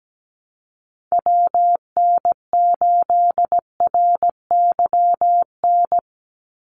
Morse code, heard as a keyed tone, about 17 wpm, 700 Hz. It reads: WN8RYN
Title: USA Call Signs 17wpm